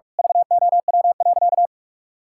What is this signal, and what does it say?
Morse code, heard as a keyed tone, about 41 wpm, 700 Hz. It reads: VOW.
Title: Pangram Challenge v2 - 45wpm